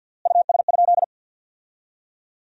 Morse code, heard as a keyed tone, about 50 wpm, 700 Hz. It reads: US?
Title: 3x Word-Spacing - Sentences from Top 400 Words - 50wpm